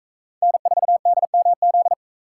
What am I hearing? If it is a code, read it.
Morse code, heard as a keyed tone, about 43 wpm, 700 Hz. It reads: N4DMZ